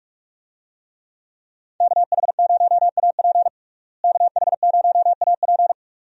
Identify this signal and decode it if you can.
Morse code, heard as a keyed tone, about 41 wpm, 700 Hz. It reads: KH0AP KH0AP
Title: USA Call Signs - 5 Characters - Repeated 3x - 45wpm